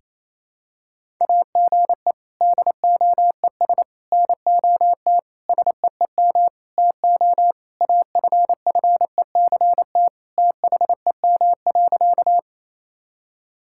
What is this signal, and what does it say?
Morse code, heard as a keyed tone, about 28 wpm, 700 Hz. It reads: AGE DOES NOT SEEM TO AFFECT THEM.
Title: Sentences from Top 1000 Words - Review Part 6 - 28wpm